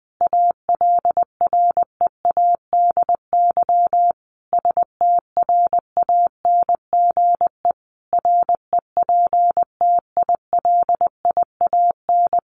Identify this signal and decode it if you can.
Morse code, heard as a keyed tone, about 20 wpm, 700 Hz. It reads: ALREADY STRANGE REPTILIAN
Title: A Sound Of Thunder 20wpm